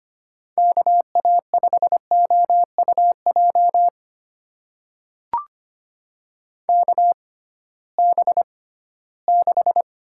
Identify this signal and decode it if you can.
Morse code, heard as a keyed tone, about 25 wpm, spaced as for 16 wpm, 700 Hz. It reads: KA5OUJ KB6
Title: USA Call Signs - 6 Characters - 6x Character Spacing - 25wpm